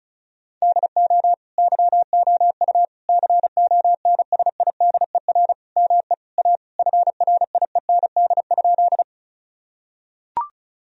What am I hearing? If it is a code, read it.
Morse code, heard as a keyed tone, about 36 wpm, 700 Hz. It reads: DO YOU CONSIDER ME A FRIEND?
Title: Sentences from Top 1000 Words - Review Part 4 - 35wpm